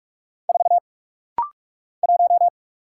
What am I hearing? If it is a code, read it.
Morse code, heard as a keyed tone, about 46 wpm, 700 Hz. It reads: 4 1
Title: Single Letter-Number - Rapid-Fire 45wpm